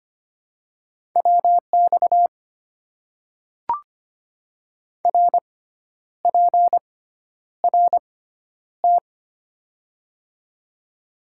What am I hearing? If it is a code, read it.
Morse code, heard as a keyed tone, about 25 wpm, spaced as for 10 wpm, 700 Hz. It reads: WX RPRT
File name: Top 90 QSO Elements - 6x Character Spacing - 25wpm